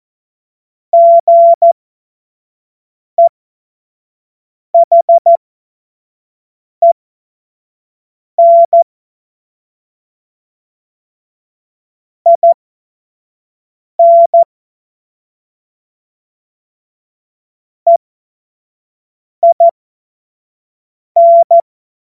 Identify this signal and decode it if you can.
Morse code, heard as a keyed tone, about 14 wpm, spaced as for 5 wpm, 680 Hz. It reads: GEHEN IN EIN
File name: Morsekurs / Witze / Witz 05. Oktober 2021, 25 F70